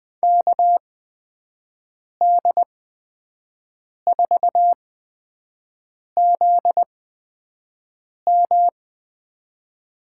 Morse code, heard as a keyed tone, about 20 wpm, spaced as for 5 wpm, 700 Hz. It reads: KD4ZM
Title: USA Call Signs - 6 Characters - 8x Character Spacing - 20wpm